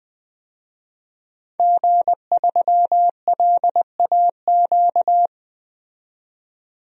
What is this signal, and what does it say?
Morse code, heard as a keyed tone, about 20 wpm, 700 Hz. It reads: G3LAQ
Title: DXCC-100 Call Signs - 5 Characters - 20wpm